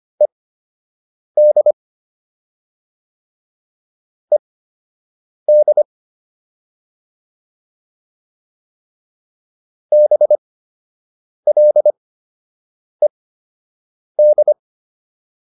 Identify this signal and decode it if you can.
Morse code, heard as a keyed tone, about 25 wpm, spaced as for 7 wpm, 600 Hz. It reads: ED ED BLED